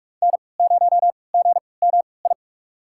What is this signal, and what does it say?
Morse code, heard as a keyed tone, about 46 wpm, spaced as for 25 wpm, 700 Hz. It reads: N0GMI
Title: USA Call Signs - 5 Characters - 3x Character Spacing - 45wpm